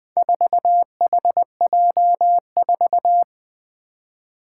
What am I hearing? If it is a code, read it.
Morse code, heard as a keyed tone, about 20 wpm, 700 Hz. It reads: 4HJ4
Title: Four Challenging Letter-Number Combinations 20wpm